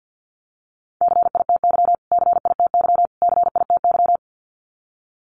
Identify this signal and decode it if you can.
Morse code, heard as a keyed tone, about 50 wpm, 700 Hz. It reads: CITY CITY CITY
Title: Top 500 Words Repeated 50wpm